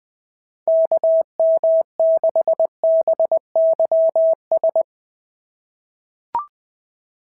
Morse code, heard as a keyed tone, about 20 wpm, 690 Hz. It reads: KM6BYS
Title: SOTA QSO Practice 20wpm